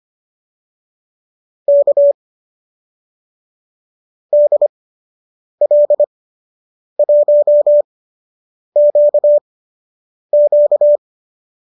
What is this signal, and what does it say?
Morse code, heard as a keyed tone, about 25 wpm, spaced as for 8 wpm, 590 Hz. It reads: K DL1QQ